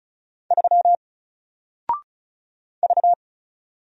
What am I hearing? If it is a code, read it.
Morse code, heard as a keyed tone, about 36 wpm, 700 Hz. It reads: 3 V
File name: Single-Letter-Number-35wpm